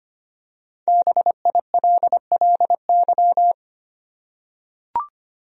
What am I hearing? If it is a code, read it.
Morse code, heard as a keyed tone, about 25 wpm, 700 Hz. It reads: BILLY
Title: USA Common Names - Max 5 Letters - 25wpm